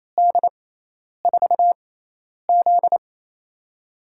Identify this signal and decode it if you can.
Morse code, heard as a keyed tone, about 28 wpm, spaced as for 10 wpm, 700 Hz. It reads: D4Z
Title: USA Call Signs - 4 Characters - 6x Character Spacing - 28wpm